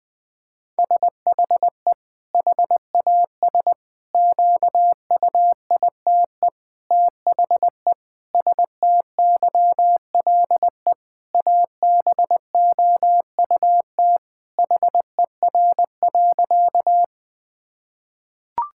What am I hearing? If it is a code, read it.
Morse code, heard as a keyed tone, about 20 wpm, 700 Hz. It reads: SHE HAS QUITE THE STYLE ABOUT HER.
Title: Sentences from Top 1000 Words - Review Part 5 - 20wpm